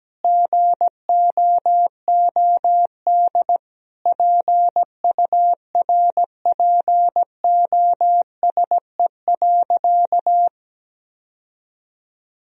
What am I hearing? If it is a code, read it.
Morse code, heard as a keyed tone, about 17 wpm, 700 Hz. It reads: GOOD PURPOSE.
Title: Sentences from Top 1000 Words - Review Part 2 - 17wpm